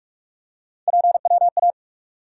Morse code, heard as a keyed tone, about 46 wpm, 700 Hz. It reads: PWA
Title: USA Call Sign Postfixes - 3 Letters 45wpm